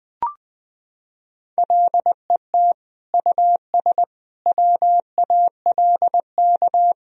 Morse code, heard as a keyed tone, about 20 wpm, 700 Hz. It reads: LET US WALK
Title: Sentences from Top 1000 Words - Review Part 3 - 20wpm